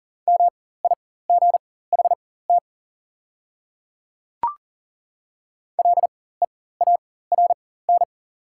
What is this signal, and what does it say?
Morse code, heard as a keyed tone, about 40 wpm, spaced as for 19 wpm, 700 Hz. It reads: MIGHT LEARN
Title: 4x Character Spacing - Top 500 Words - 40wpm